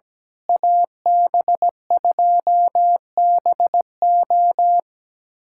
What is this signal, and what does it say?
Morse code, heard as a keyed tone, about 17 wpm, 700 Hz. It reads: AB2BO
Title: USA Callsign DE Callsign - 5 Characters - 17wpm